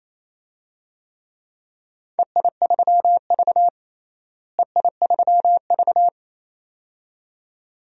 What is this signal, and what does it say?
Morse code, heard as a keyed tone, about 28 wpm, 700 Hz. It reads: EI3V EI3V
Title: DXCC-100 Call Signs - 4 Characters - Repeated 2x - 28wpm